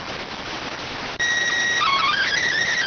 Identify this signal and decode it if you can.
SSTV (slow-scan television), PD 290.